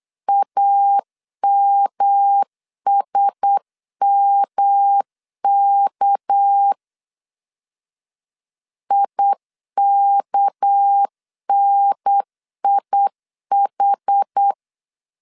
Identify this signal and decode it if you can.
Morse code, heard as a keyed tone, about 8 wpm, 800 Hz. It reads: AMSMK IKNIH
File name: CW / DET / 1 6 1500C S B 8wpm 5142